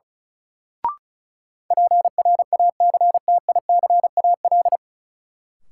Morse code, heard as a keyed tone, about 36 wpm, 700 Hz. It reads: PRACTICAL
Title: Sentences from Top 1400 Words - New Word Drill Alt - 35wpm